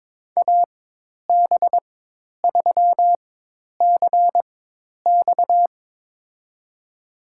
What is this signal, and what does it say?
Morse code, heard as a keyed tone, about 21 wpm, spaced as for 10 wpm, 700 Hz. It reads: AB3CX
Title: USA Call Signs - 5 Characters - 4x Character Spacing - 22wpm